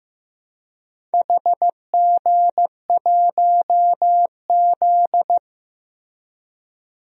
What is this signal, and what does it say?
Morse code, heard as a keyed tone, about 15 wpm, 700 Hz. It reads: HG1Z
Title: DXCC-100 Call Signs - 4 Characters - 15wpm